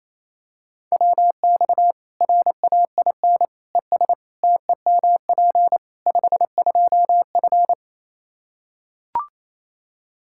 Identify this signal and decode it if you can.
Morse code, heard as a keyed tone, about 28 wpm, 700 Hz. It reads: WX RAIN ES TEMP 52F
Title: Practice QSO - Exchange 2 - Info - 28wpm